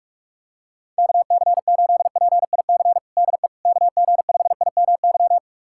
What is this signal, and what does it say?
Morse code, heard as a keyed tone, about 46 wpm, 700 Hz. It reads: KC8PIC DE KG5IMY
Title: USA Callsign DE Callsign - 6 Characters - 45wpm